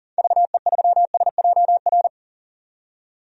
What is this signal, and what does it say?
Morse code, heard as a keyed tone, about 40 wpm, 700 Hz. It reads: VE3SJR
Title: DXCC-100 Call Signs - 6 Characters - 40wpm